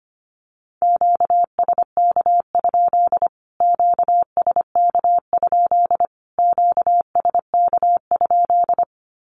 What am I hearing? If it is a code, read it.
Morse code, heard as a keyed tone, about 25 wpm, 700 Hz. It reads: QSK? QSK? QSK?
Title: Q-Codes Repeated 3x 25wpm